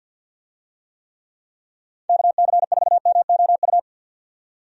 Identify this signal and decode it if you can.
Morse code, heard as a keyed tone, about 48 wpm, 700 Hz. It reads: KC4MOU
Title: USA Call Signs - 6 Characters - 50wpm